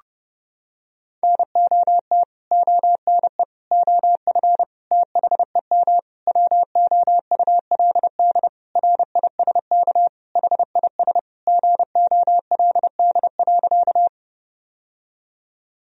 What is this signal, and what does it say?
Morse code, heard as a keyed tone, about 32 wpm, 700 Hz. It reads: NOT ONE OF THEM WOULD RISK HIS GOLD.
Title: Sentences from Top 1000 Words - Review Part 4 - 30wpm